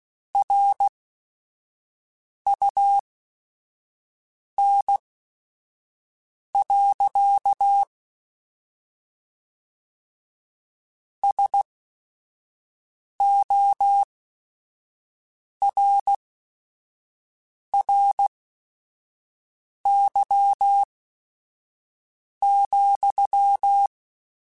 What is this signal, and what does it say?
Morse code, heard as a keyed tone, about 16 wpm, spaced as for 5 wpm, 800 Hz. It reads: RUN. SORRY,